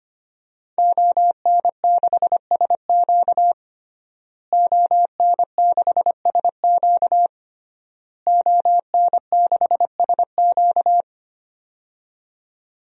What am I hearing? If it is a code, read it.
Morse code, heard as a keyed tone, about 25 wpm, 700 Hz. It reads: ON6SQ ON6SQ ON6SQ
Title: DXCC-100 Call Signs - 5 Characters - Repeated 3x - 25wpm